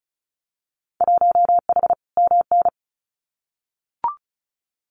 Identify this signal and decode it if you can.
Morse code, heard as a keyed tone, about 36 wpm, 700 Hz. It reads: 1H MN
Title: Winter Field Day Exchanges Encore 35wpm